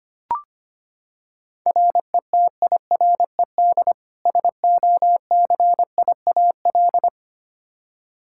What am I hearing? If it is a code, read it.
Morse code, heard as a keyed tone, about 25 wpm, 700 Hz. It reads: RETIRED SOCIAL
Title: Practice QSO - Exchange 3 - More Info - 25wpm